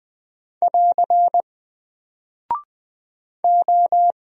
Morse code, heard as a keyed tone, about 20 wpm, 700 Hz. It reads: <AR> O
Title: Single Character (Letter:Number:Punctuation:Prosign) 20wpm